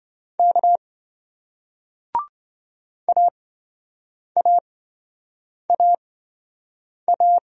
Morse code, heard as a keyed tone, about 28 wpm, spaced as for 7 wpm, 700 Hz. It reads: K AAAA